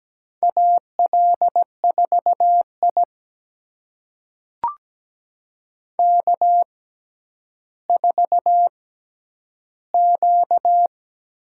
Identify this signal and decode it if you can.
Morse code, heard as a keyed tone, about 17 wpm, spaced as for 10 wpm, 700 Hz. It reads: AL4I K4Q
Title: USA Call Signs - 4 Characters - 6x Character Spacing - 17wpm